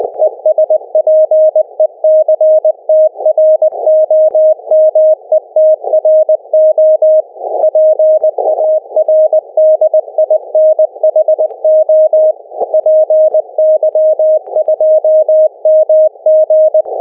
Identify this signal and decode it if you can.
Morse code, heard as a keyed tone, about 20 wpm, 620 Hz. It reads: ESPECTROMETRO PARDINHO PY2MG
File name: py2mg / Arquivos PDF / Transceptor fenix RX PArdinho